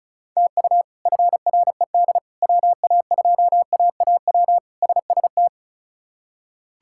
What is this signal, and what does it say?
Morse code, heard as a keyed tone, about 36 wpm, 700 Hz. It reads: TU FRED WA2AAW SST
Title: K1USN SST QSO Practice - 35wpm